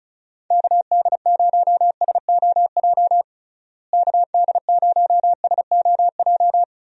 Morse code, heard as a keyed tone, about 34 wpm, 700 Hz. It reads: KD0SOJ KD0SOJ
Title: USA Call Signs - 6 Characters - Repeated 2x - 35wpm